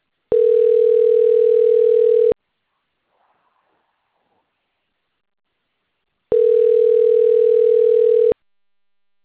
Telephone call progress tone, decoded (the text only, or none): ringback tone